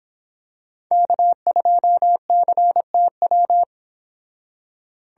Morse code, heard as a keyed tone, about 25 wpm, 700 Hz. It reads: K2CTW